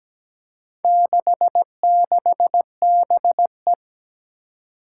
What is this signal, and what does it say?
Morse code, heard as a keyed tone, about 17 wpm, 700 Hz. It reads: 66BE
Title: Four Challenging Letter-Number Combinations 17wpm